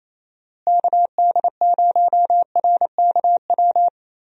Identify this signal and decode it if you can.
Morse code, heard as a keyed tone, about 28 wpm, 700 Hz. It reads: KD0RKW
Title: USA Call Signs - 6 Characters - Repeated 3x - 28wpm